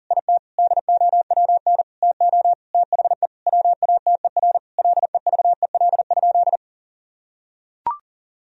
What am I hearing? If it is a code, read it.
Morse code, heard as a keyed tone, about 40 wpm, 700 Hz. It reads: IT DOWN TO THE WATER LEVEL?
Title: Sentences from Top 1000 Words - Review Part 4 - 40wpm